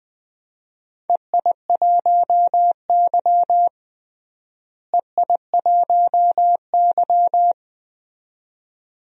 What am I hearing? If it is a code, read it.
Morse code, heard as a keyed tone, about 21 wpm, 700 Hz. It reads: EI1Y EI1Y
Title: DXCC-100 Call Signs - 4 Characters - Repeated 2x - 20wpm